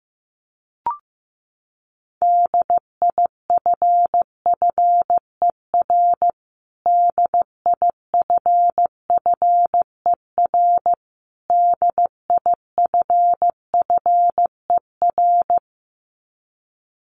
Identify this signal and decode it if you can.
Morse code, heard as a keyed tone, about 15 wpm, 700 Hz. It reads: DIFFER DIFFER DIFFER
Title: Top 200 Words Repeated 15wpm